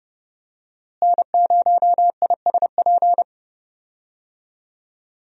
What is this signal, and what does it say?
Morse code, heard as a keyed tone, about 30 wpm, 700 Hz. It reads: N0ISP
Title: USA Call Signs - 5 Characters - Repeated 3x - 30wpm